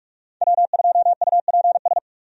Morse code, heard as a keyed tone, about 42 wpm, 700 Hz. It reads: W2UPS
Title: USA Call Signs - 5 Characters - 45wpm